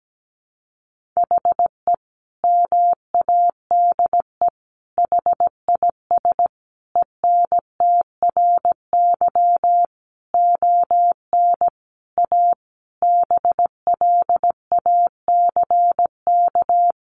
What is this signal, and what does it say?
Morse code, heard as a keyed tone, about 17 wpm, 700 Hz. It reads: HE MADE HIS ENTRY ON A BLACK
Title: Sentences from Top 1500 Words 17wpm